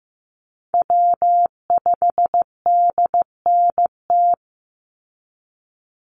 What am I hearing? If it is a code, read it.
Morse code, heard as a keyed tone, about 15 wpm, 700 Hz. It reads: W5DNT
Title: Call Signs Repeated 2x - 15wpm